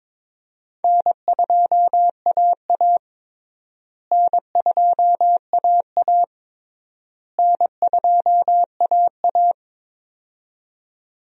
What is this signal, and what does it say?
Morse code, heard as a keyed tone, about 22 wpm, 700 Hz. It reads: N2AA N2AA N2AA
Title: USA Call Signs - 4 Characters - Repeated 3x - 22wpm